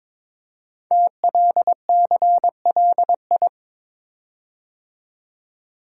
Morse code, heard as a keyed tone, about 22 wpm, 700 Hz. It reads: TLCLI
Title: Code Groups - 5 Letter:Numbers - 22wpm